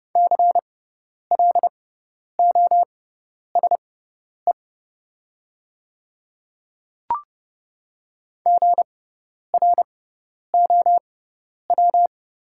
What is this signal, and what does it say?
Morse code, heard as a keyed tone, about 30 wpm, spaced as for 10 wpm, 700 Hz. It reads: CLOSE GROW
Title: 6x Character Spacing - Top 500 Words - 30wpm